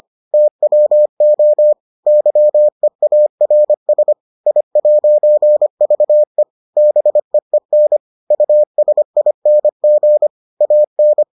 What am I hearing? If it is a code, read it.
Morse code, heard as a keyed tone, about 25 wpm, 600 Hz. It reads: TWO YEARS I'VE BEEN USING AN